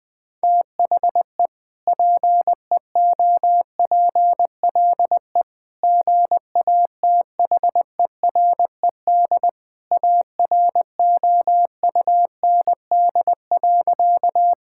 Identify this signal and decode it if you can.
Morse code, heard as a keyed tone, about 20 wpm, 700 Hz. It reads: THE PEOPLE GATHERED AROUND.
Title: Sentences with 2-Word Phrases - 20wpm